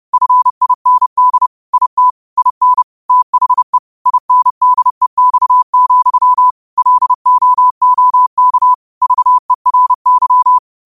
Morse code, heard as a keyed tone, about 30 wpm, 1000 Hz. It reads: FIND IT IN THE INDEX, LOOK VERY